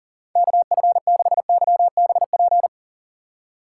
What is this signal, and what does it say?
Morse code, heard as a keyed tone, about 40 wpm, 700 Hz. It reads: KF6YBP
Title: USA Call Signs - 2x Character Spacing - 40wpm